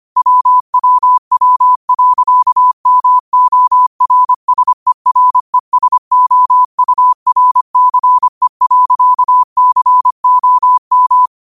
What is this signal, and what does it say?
Morse code, heard as a keyed tone, about 25 wpm, 1000 Hz. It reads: WWW.MORSERESOURCE.COM
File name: qotdfiles / 25wpm